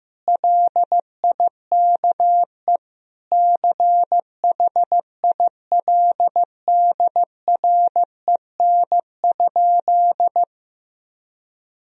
Morse code, heard as a keyed tone, about 16 wpm, 700 Hz. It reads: LIKE CHILDREN?
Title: Sentences from Top 1000 Words - Review Part 2 - 15wpm